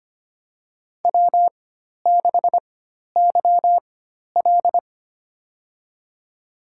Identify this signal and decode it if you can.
Morse code, heard as a keyed tone, about 25 wpm, spaced as for 12 wpm, 700 Hz. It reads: W6YL
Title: USA Call Signs - 4 Characters - 4x Character Spacing - 25wpm